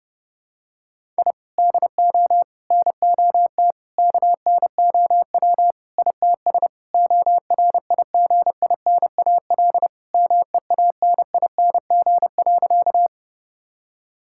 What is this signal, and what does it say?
Morse code, heard as a keyed tone, about 30 wpm, 700 Hz. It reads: I DO NOT KNOW ITS ORIGINAL MEANING.
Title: Sentences from Top 1000 Words - Review Part 6 - 30wpm